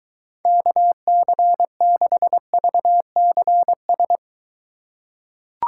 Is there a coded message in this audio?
Yes: Morse code from 0.5 to 4.2 s, heard as a keyed tone, about 22 wpm, 700 Hz. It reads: KC6VCS